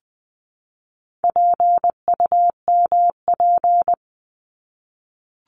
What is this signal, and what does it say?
Morse code, heard as a keyed tone, about 20 wpm, 700 Hz. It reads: PUMP
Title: 260 Lesson 42e - Words 20-20wpm